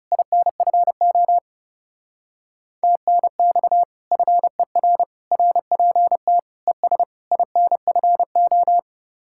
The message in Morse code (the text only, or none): INFO TNX FER RPT ES INFO